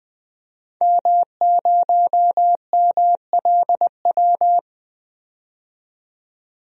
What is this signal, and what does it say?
Morse code, heard as a keyed tone, about 21 wpm, 700 Hz. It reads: M0MLW
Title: DXCC-100 Call Signs - 5 Characters - 20wpm